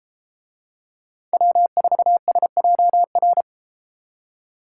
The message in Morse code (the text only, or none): W4SJR